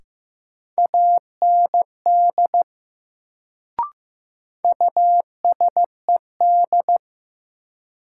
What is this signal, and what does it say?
Morse code, heard as a keyed tone, about 15 wpm, 700 Hz. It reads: AND USED
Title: Sets of 2 Words - New Word Drill - 15wpm